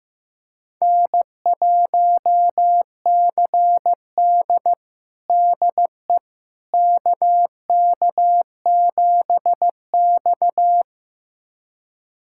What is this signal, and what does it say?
Morse code, heard as a keyed tone, about 16 wpm, 700 Hz. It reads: N1CD DE KK7X
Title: USA Callsign DE Callsign - 4 Characters - 15wpm